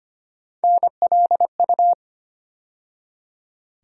Morse code, heard as a keyed tone, about 25 wpm, 700 Hz. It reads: NLU
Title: USA Call Sign Postfixes - 3 Letters 25wpm